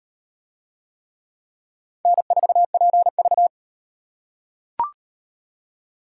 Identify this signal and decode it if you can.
Morse code, heard as a keyed tone, about 38 wpm, 700 Hz. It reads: N4PV